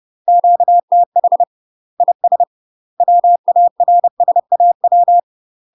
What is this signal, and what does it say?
Morse code, heard as a keyed tone, about 30 wpm, 700 Hz. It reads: QTH IS WARSAW
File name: UK MASTER QSOs TRAINING / 30WPM / UK MASTER QSOs 0044